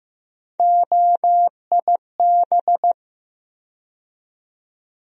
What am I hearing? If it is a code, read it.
Morse code, heard as a keyed tone, about 15 wpm, 700 Hz. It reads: OIB
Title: USA Call Sign Postfixes - 3 Letters 15wpm